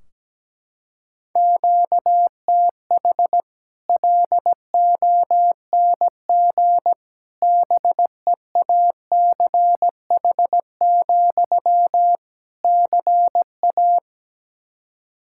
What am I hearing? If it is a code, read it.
Morse code, heard as a keyed tone, about 17 wpm, 700 Hz. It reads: QTH LONG BEACH, CA
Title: Practice QSO - Exchange 1 - Beginning - 17wpm